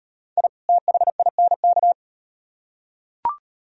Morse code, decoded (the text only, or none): I THINK